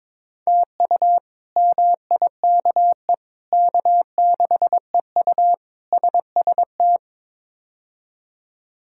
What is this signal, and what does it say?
Morse code, heard as a keyed tone, about 22 wpm, 700 Hz. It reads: TU MIKE K6EU SST
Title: K1USN SST QSO Practice - 22wpm